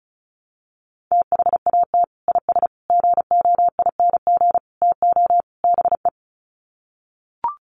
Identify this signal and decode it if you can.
Morse code, heard as a keyed tone, about 34 wpm, 700 Hz. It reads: THAT IS GOING TO BE
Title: Sets of 5 Words Encore2 35wpm